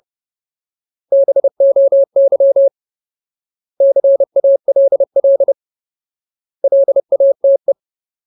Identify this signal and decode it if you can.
Morse code, heard as a keyed tone, about 30 wpm, 550 Hz. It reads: BOY CALL LATE
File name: Ultimate Word List / 30WPM Ultimate